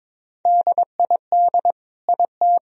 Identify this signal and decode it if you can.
Morse code, heard as a keyed tone, about 21 wpm, 700 Hz. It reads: DID IT
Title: Sentences from Top 1000 Words - Review Part 2 - 22wpm